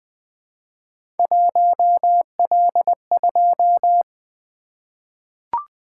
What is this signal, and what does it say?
Morse code, heard as a keyed tone, about 20 wpm, 700 Hz. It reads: 1L2
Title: Three Letter-Number Combinations - 20wpm